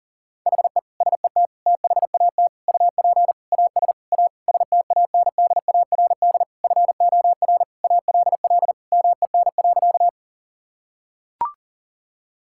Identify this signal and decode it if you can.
Morse code, heard as a keyed tone, about 40 wpm, 700 Hz. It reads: HE SET THAT UP AS A STANDARD FOR ALL MEN.
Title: Sentences from Top 1000 Words - Review Part 6 - 40wpm